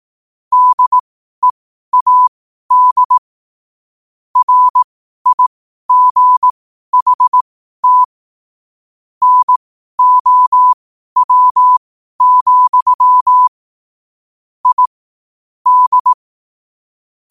Morse code, heard as a keyed tone, about 18 wpm, spaced as for 12 wpm, 1000 Hz. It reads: DEAD RIGHT NOW, I D
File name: qotdfiles / 12wpm